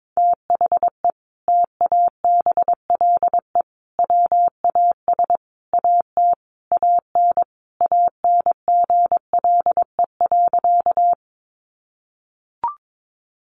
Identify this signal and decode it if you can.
Morse code, heard as a keyed tone, about 22 wpm, 700 Hz. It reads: THE TABLE WAS AT AN ANGLE.
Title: Sentences from Top 1500 Words 22wpm